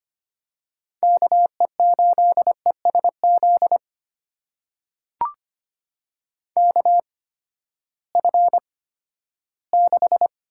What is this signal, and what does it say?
Morse code, heard as a keyed tone, about 25 wpm, spaced as for 14 wpm, 700 Hz. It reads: KE8ESZ KF6
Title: USA Call Signs - 6 Characters - 8x Character Spacing - 25wpm